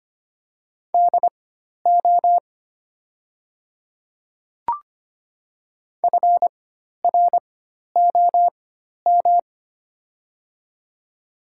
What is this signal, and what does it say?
Morse code, heard as a keyed tone, about 25 wpm, spaced as for 12 wpm, 690 Hz. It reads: DO FROM
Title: 4x Character Spacing - Top 300 Words - 25wpm